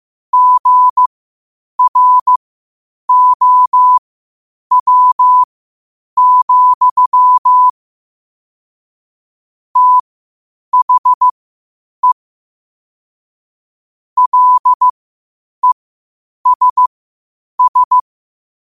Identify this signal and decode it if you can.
Morse code, heard as a keyed tone, about 15 wpm, spaced as for 8 wpm, 1000 Hz. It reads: GROW, THE LESS
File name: qotdfiles / 8wpm